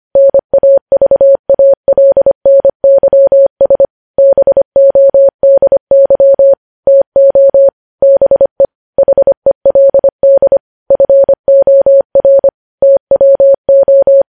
Morse code, heard as a keyed tone, about 25 wpm, 550 Hz. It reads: NAVALNYS BODY TO BE HELD FOR TWO
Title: News Headlines at 25